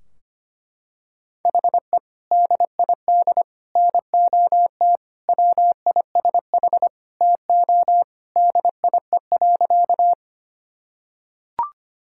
Morse code, heard as a keyed tone, about 25 wpm, 700 Hz. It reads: HE DID NOT WISH TO DIE.
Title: Sentences from Top 1600 Words 25wpm